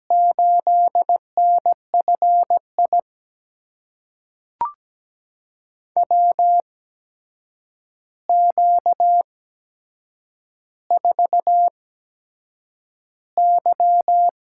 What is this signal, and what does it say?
Morse code, heard as a keyed tone, about 17 wpm, spaced as for 7 wpm, 700 Hz. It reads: 8NFI WQ4Y
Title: USA Call Signs - 8x Character Spacing - 17wpm